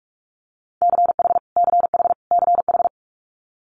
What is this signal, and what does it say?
Morse code, heard as a keyed tone, about 46 wpm, 700 Hz. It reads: CH CH CH
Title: Multi-letter Phonemes Repeated 3x 45wpm